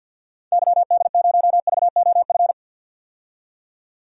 Morse code, heard as a keyed tone, about 50 wpm, 700 Hz. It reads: YD0VOF
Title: DXCC-100 Call Signs - 6 Characters - 50wpm